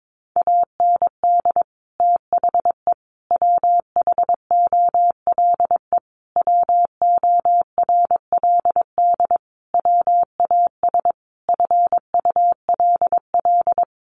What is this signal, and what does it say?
Morse code, heard as a keyed tone, about 22 wpm, 700 Hz. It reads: AND THE WHOLE WORLD WAS FULL
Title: Sentences from Top 1300 Words 22wpm